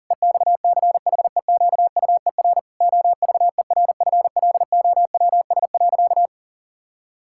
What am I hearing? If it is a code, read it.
Morse code, heard as a keyed tone, about 40 wpm, 700 Hz. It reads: EXCHEQUER OVERFLOWS.
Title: Pangram Challenge v2 - 40wpm